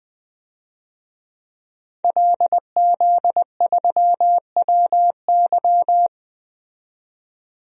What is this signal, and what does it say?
Morse code, heard as a keyed tone, about 21 wpm, 700 Hz. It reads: LZ3WY
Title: DXCC-100 Call Signs - 5 Characters - 20wpm